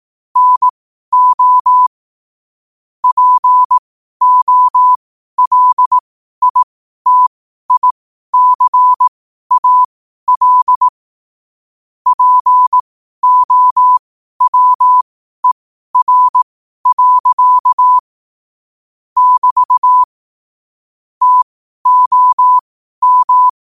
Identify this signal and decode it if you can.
Morse code, heard as a keyed tone, about 18 wpm, spaced as for 12 wpm, 1000 Hz. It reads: NO POLITICAL POWER. <BT> TOM
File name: qotdfiles / 12wpm